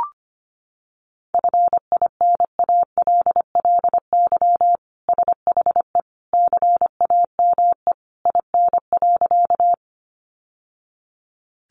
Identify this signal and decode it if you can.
Morse code, heard as a keyed tone, about 25 wpm, 700 Hz. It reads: FINALLY SHE CAME IN.
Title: Sentences from Top 1000 Words 25wpm